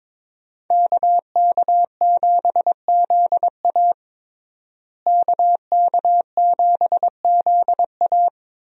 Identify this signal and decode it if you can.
Morse code, heard as a keyed tone, about 21 wpm, 700 Hz. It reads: KK7ZA KK7ZA
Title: USA Call Signs - 5 Characters - Repeated 3x - 22wpm